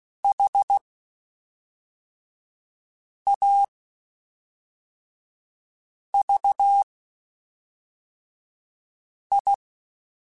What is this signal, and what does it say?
Morse code, heard as a keyed tone, about 16 wpm, spaced as for 3 wpm, 800 Hz. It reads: HAVI
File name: Quotes Spaced / Quotes Spaced-mp3s / ChildAsleep sp @5